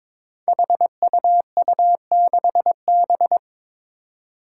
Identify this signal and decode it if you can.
Morse code, heard as a keyed tone, about 22 wpm, 700 Hz. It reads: HUU6B
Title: Code Groups - 5 Letter:Numbers - 22wpm